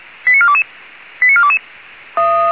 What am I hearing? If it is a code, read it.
Five-tone selective call (ZVEI), heard as quick runs of five tones, tones of about 70 ms. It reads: 89422 89422